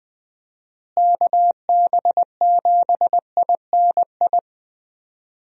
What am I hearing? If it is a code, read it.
Morse code, heard as a keyed tone, about 20 wpm, 700 Hz. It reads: KB7INI
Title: USA Call Signs - 6 Characters - Repeated 2x - 20wpm